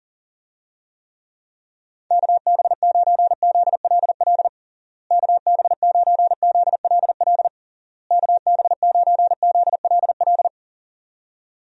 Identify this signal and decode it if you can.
Morse code, heard as a keyed tone, about 37 wpm, 700 Hz. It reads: KB9ZLL KB9ZLL KB9ZLL
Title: USA Call Signs - 6 Characters - Repeated 3x - 40wpm